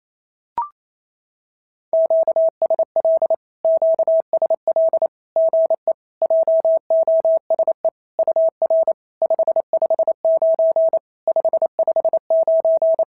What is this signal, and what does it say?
Morse code, heard as a keyed tone, about 27 wpm, 650 Hz. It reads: QSL QSL GE JOSE UR 559 559
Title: SOTA QSO Practice 28wpm